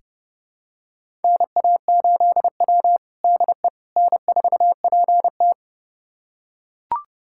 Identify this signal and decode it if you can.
Morse code, heard as a keyed tone, about 31 wpm, 700 Hz. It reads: NA8W DE N4PT
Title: USA Callsign DE Callsign - 4 Characters - 30wpm